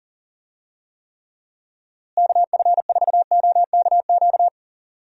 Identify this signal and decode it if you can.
Morse code, heard as a keyed tone, about 39 wpm, 700 Hz. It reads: KF4OKQ
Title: USA Call Signs - 6 Characters - Repeated 3x - 40wpm